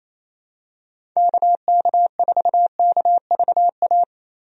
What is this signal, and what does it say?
Morse code, heard as a keyed tone, about 28 wpm, 700 Hz. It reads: KK4KVA
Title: USA Call Signs - 6 Characters - Repeated 2x - 28wpm